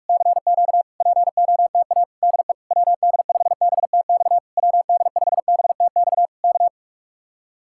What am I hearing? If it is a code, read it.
Morse code, heard as a keyed tone, about 44 wpm, 700 Hz. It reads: CQ POTA DE WD5BTX WD5BTX K
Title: Speed-Racing Common QSO Phrases - 45wpm